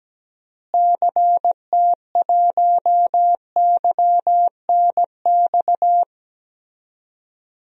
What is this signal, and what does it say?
Morse code, heard as a keyed tone, about 17 wpm, 700 Hz. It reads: CT1YNX
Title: DXCC-100 Call Signs - 6 Characters - Repeated 3x - 17wpm